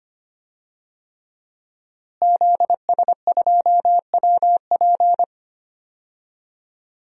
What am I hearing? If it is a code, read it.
Morse code, heard as a keyed tone, about 25 wpm, 700 Hz. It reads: ZS2WP
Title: DXCC-100 Call Signs - 5 Characters - Repeated 3x - 25wpm